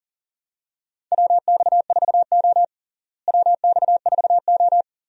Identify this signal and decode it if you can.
Morse code, heard as a keyed tone, about 38 wpm, 700 Hz. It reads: WX4O WX4O
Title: USA Call Signs - 4 Characters - Repeated 2x - 40wpm